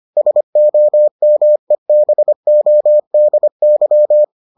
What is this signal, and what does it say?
Morse code, heard as a keyed tone, about 25 wpm, 600 Hz. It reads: SOMEBODY